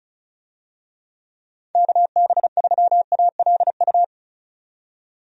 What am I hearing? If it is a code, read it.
Morse code, heard as a keyed tone, about 34 wpm, 700 Hz. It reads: KB3ALU
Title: USA Call Signs - 6 Characters - 35wpm